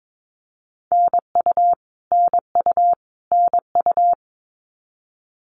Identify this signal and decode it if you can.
Morse code, heard as a keyed tone, about 22 wpm, 700 Hz. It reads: NU NU NU
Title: Canadian Provinces and Territories - Repeated 3x - 22wpm